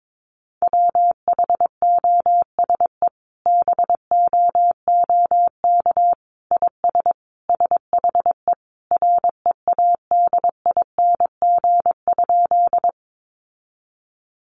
Morse code, heard as a keyed tone, about 22 wpm, 700 Hz. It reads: WHOSE BOOK IS SHE READING?
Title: Sentences from Top 1200 Words 22wpm